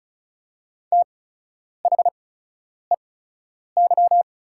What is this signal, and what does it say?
Morse code, heard as a keyed tone, about 36 wpm, spaced as for 10 wpm, 690 Hz. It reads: THEY
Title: 8x Character Spacing - Top 500 Words - 35wpm